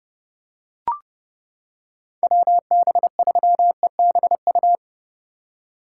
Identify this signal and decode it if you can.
Morse code, heard as a keyed tone, about 31 wpm, 700 Hz. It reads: WB3EBU